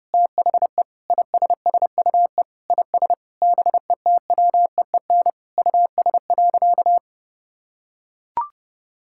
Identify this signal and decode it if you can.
Morse code, heard as a keyed tone, about 32 wpm, 700 Hz. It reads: THE ISSUE IS BETWEEN US.
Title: Sentences from Top 1000 Words - Review Part 6 - 30wpm